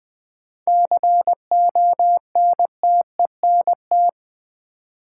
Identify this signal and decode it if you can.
Morse code, heard as a keyed tone, about 20 wpm, 700 Hz. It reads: CONTENT
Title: Sentences from Top 900 Words - New Word Drill - 20wpm